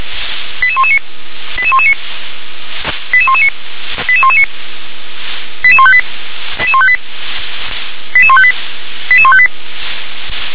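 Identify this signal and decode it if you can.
Five-tone selective call (ZVEI), heard as quick runs of five tones, tones of about 70 ms. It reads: 88119 88119 88119 88119 88158 88158 88158 88158
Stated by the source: ZVEI five-tone call 88020 received by SDR.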